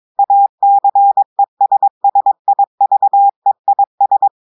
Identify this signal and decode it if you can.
Morse code, heard as a keyed tone, about 22 wpm, 800 Hz. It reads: ACESSIVEIS
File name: audios / portugais / portugais texte en clair 1 vitesse 22 mots minutes